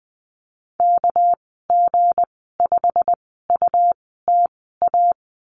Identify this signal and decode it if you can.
Morse code, heard as a keyed tone, about 19 wpm, spaced as for 14 wpm, 700 Hz. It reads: KG5UTA
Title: 213 Lesson 34g - Number 8 - Call Signs 20wpm